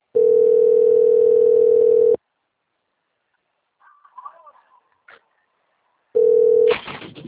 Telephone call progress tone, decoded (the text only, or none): ringback tone